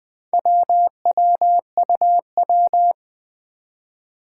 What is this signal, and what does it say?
Morse code, heard as a keyed tone, about 20 wpm, 700 Hz. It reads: WWUW
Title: Four Challenging Letter-Number Combinations 20wpm